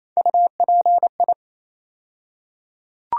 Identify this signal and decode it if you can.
Morse code, heard as a keyed tone, about 28 wpm, 700 Hz. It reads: UPI
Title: USA Call Sign Postfixes - 3 Letters 28wpm